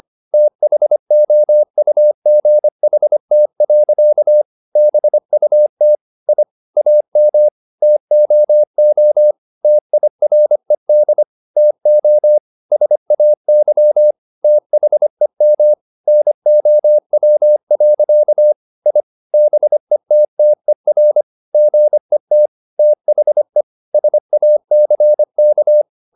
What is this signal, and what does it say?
Morse code, heard as a keyed tone, about 25 wpm, 600 Hz. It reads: THOUGHT. BUT I AM TOO TIRED TO SAY THEM NOW. I BETTER GET THE SACK